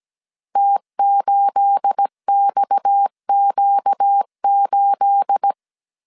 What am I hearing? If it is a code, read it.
Morse code, heard as a keyed tone, about 17 wpm, 800 Hz. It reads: T8XQ8
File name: CW / DET / 1 9 1500C S C 16wpm 2909